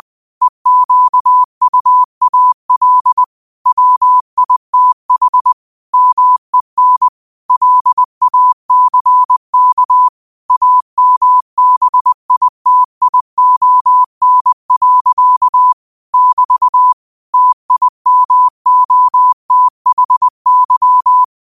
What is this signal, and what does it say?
Morse code, heard as a keyed tone, about 20 wpm, 1000 Hz. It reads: EQUAL WITH MEN LACK AMBITION. <BT> TIMOTHY
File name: qotdfiles / 20wpm